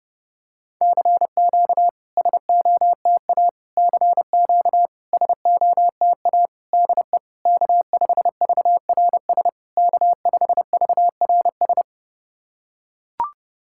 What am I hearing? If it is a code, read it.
Morse code, heard as a keyed tone, about 30 wpm, 690 Hz. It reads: CQ SOTA CQ SOTA DE K5VRS K5VRS
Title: SOTA QSO Practice 30wpm